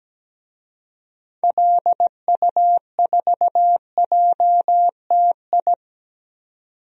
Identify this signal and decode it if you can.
Morse code, heard as a keyed tone, about 17 wpm, 700 Hz. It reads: LU4JTI
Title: DXCC-100 Call Signs - 6 Characters - 17wpm